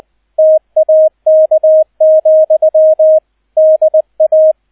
Morse code, heard as a keyed tone, about 20 wpm, 630 Hz, with radio noise. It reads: TAK, DA
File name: Erster Streich 20wpm